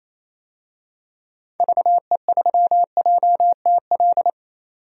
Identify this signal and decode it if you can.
Morse code, heard as a keyed tone, about 28 wpm, 700 Hz. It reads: VE3JTL
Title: DXCC-100 Call Signs - 6 Characters - 28wpm